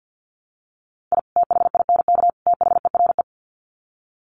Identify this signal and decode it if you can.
Morse code, heard as a keyed tone, about 48 wpm, 700 Hz. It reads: I THINK THERE
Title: Sets of 3 Words 50wpm